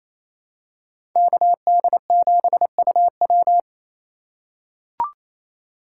Morse code, heard as a keyed tone, about 28 wpm, 700 Hz. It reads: KD7UW
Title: USA Call Signs - 5 Characters - 4x Character Spacing - 28wpm